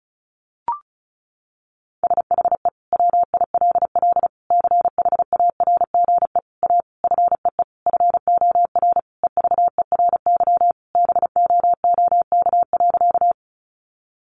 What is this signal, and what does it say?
Morse code, heard as a keyed tone, about 34 wpm, 700 Hz. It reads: SHE WILL CHARGE A FEE FOR EVERY BOOK.
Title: Sentences from Top 1500 Words 35wpm